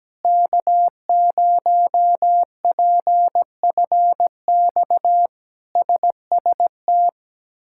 Morse code, heard as a keyed tone, about 17 wpm, 700 Hz. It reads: K0PFX SST
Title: K1USN SST QSO Practice - 17wpm